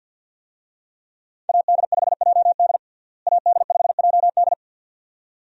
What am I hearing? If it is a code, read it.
Morse code, heard as a keyed tone, about 48 wpm, 700 Hz. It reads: AD5JD AD5JD
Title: USA Call Signs - 5 Characters - Repeated 2x - 50wpm